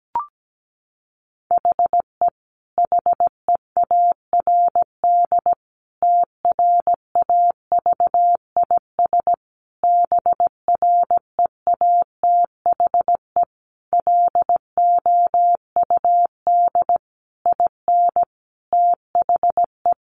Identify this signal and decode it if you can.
Morse code, heard as a keyed tone, about 17 wpm, 700 Hz. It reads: HE HEARD TRAVIS BREATHE LOUD IN THE
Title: A Sound Of Thunder 17wpm